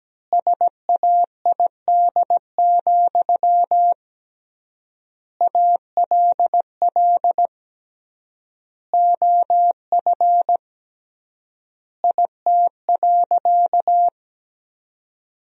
Morse code, heard as a keyed tone, about 17 wpm, 700 Hz. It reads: SAID, ALL OF IT.
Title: 3x Word-Spacing - Sentences from Top 500 Words - 17wpm